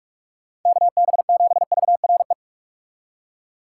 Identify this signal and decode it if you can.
Morse code, heard as a keyed tone, about 42 wpm, 700 Hz. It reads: KB7VRE
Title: USA Call Signs - 6 Characters - 2x Character Spacing - 45wpm